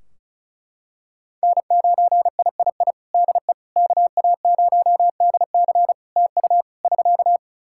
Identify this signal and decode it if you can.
Morse code, heard as a keyed tone, about 36 wpm, 700 Hz. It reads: N9III DE KA0DC TU <SK>
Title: Practice QSO - Exchange 4 - Ending - 35wpm